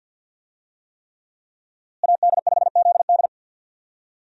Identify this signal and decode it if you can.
Morse code, heard as a keyed tone, about 51 wpm, 700 Hz. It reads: AD5ZD
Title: USA Call Signs - 5 Characters - Repeated 3x - 50wpm